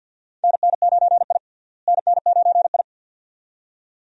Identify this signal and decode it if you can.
Morse code, heard as a keyed tone, about 50 wpm, 700 Hz. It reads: NN9I NN9I
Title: USA Call Signs - 4 Characters - Repeated 2x - 50wpm